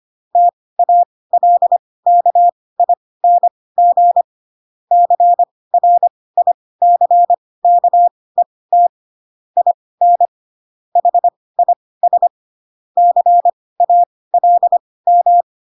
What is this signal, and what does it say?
Morse code, heard as a keyed tone, about 25 wpm, spaced as for 18 wpm, 700 Hz. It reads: TALKING CRICKET IN HIS CALM
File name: Pinocchio / Pinocchio ch4 12 25 ik1hge